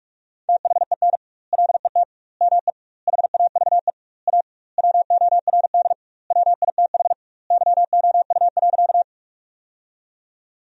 Morse code, heard as a keyed tone, about 46 wpm, 700 Hz. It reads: THEN LET ME HAVE A WORD WITH YOU.
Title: 2x Word-Spacing - Sentences from Top 300 Words - 45wpm